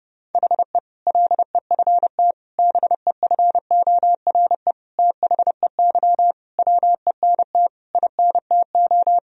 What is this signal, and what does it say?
Morse code, heard as a keyed tone, about 28 wpm, 700 Hz. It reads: HE LEFT BEFORE THEY WENT INTO
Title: Sentences from Top 1000 Words - Review Part 4 - 30wpm